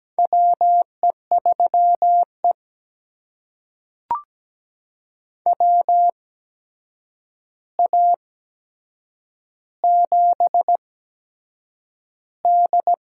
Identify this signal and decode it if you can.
Morse code, heard as a keyed tone, about 17 wpm, spaced as for 7 wpm, 700 Hz. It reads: WE3E WA7D
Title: USA Call Signs - 4 Characters - 8x Character Spacing - 17wpm